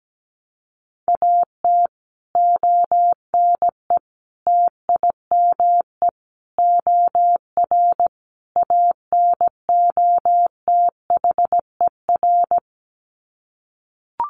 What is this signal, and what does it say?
Morse code, heard as a keyed tone, about 17 wpm, 700 Hz. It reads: AT ONE TIME OR ANOTHER
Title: Sets of 5 Words Encore 17wpm